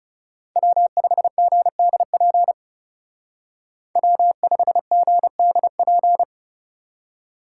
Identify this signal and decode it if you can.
Morse code, heard as a keyed tone, about 37 wpm, 700 Hz. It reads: W5GDP W5GDP
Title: Speed Racing - USA 1x3 Call Signs 30wpm